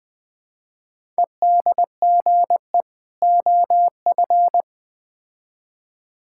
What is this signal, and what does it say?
Morse code, heard as a keyed tone, about 21 wpm, 700 Hz. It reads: EDGE OF
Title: Build A Phrase - 3-Word Phrases - 20wpm